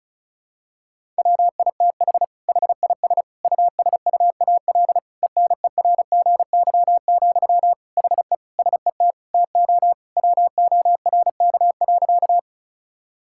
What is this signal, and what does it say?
Morse code, heard as a keyed tone, about 34 wpm, 700 Hz. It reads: WITH HIS USUAL ENERGY, HE SET TO WORK.
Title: Sentences from Top 1000 Words - Review Part 6 - 35wpm